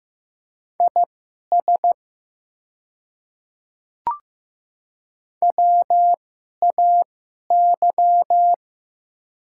Morse code, heard as a keyed tone, about 15 wpm, spaced as for 11 wpm, 700 Hz. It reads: IS WAY
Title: 2x Character Spacing - Top 100 Words - 15wpm